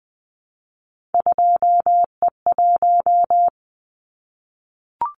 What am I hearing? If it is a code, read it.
Morse code, heard as a keyed tone, about 20 wpm, 700 Hz. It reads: 2E1
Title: Callsign Prefixes for Top 25 DXCC - Repeated 3x - 20wpm